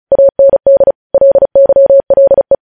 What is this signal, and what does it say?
Morse code, heard as a keyed tone, about 36 wpm, 550 Hz. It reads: AND LYLE
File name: podcast / News headlines-1x-35x35